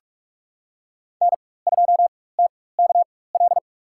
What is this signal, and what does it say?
Morse code, heard as a keyed tone, about 46 wpm, spaced as for 21 wpm, 700 Hz. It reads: N2TKL
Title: USA Call Signs - 5 Characters - 4x Character Spacing - 45wpm